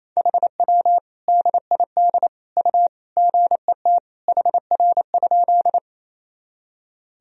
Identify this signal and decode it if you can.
Morse code, heard as a keyed tone, about 28 wpm, 700 Hz. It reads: HW DID U GET HR?
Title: Sentences from Top 200 Words and Abbreviations 28wpm